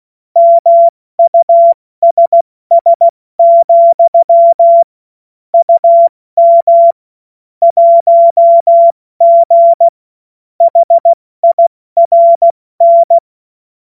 Morse code, heard as a keyed tone, about 16 wpm, spaced as for 14 wpm, 680 Hz. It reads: MUSS, UM 1G HIRN
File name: Morsekurs / Witze / Witz 07. Juni 2022 70 F80